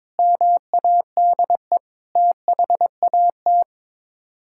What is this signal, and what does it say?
Morse code, heard as a keyed tone, about 22 wpm, 700 Hz. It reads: MADE THAT
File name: practice / 22wpm / Sets of 8 Words - 22wpm